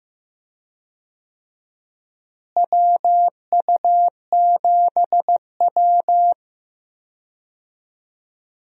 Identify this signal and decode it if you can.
Morse code, heard as a keyed tone, about 16 wpm, 700 Hz. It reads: WU7W
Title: USA Call Signs - 4 Characters - Repeated 3x - 15wpm